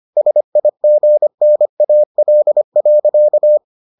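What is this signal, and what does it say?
Morse code, heard as a keyed tone, about 25 wpm, 600 Hz. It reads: SIGNAL.
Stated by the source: Morse edition of the Foundations of Amateur Radio episodes.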